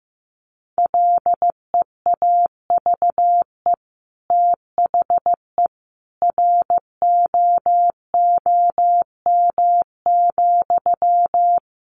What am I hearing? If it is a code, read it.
Morse code, heard as a keyed tone, about 15 wpm, 700 Hz. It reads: LEAVE THE ROOM,
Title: Sentences from Top 1500 Words 15wpm